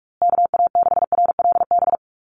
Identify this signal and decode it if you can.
Morse code, heard as a keyed tone, about 44 wpm, 700 Hz. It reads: KA6RLB
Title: USA Call Signs Repeated 3x 45wpm